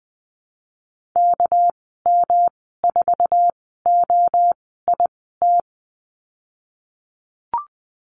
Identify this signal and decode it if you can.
Morse code, heard as a keyed tone, about 19 wpm, spaced as for 14 wpm, 700 Hz. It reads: KM4OIT
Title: 238 Lesson 38g - Letter q - Call Signs 20wpm